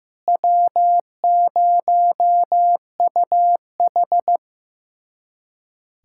Morse code, heard as a keyed tone, about 15 wpm, 700 Hz. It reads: W0UH
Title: USA Callsign DE Callsign - 4 Characters - 15wpm